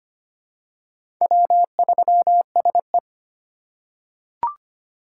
Morse code, heard as a keyed tone, about 25 wpm, 700 Hz. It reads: W3SE
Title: USA Call Signs - 4 Characters - 6x Character Spacing - 25wpm